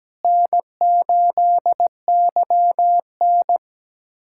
Morse code, heard as a keyed tone, about 17 wpm, 700 Hz. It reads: N8YN
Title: USA Callsign DE Callsign - 4 Characters - 17wpm